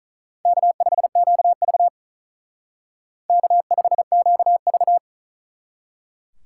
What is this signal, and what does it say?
Morse code, heard as a keyed tone, about 45 wpm, 700 Hz. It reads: K5QV K5QV